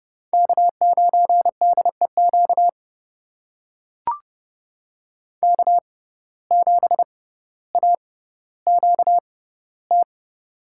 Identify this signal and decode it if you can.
Morse code, heard as a keyed tone, about 30 wpm, spaced as for 15 wpm, 700 Hz. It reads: K9DEQ K7AQT
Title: USA Call Signs - 5 Characters - 6x Character Spacing - 30wpm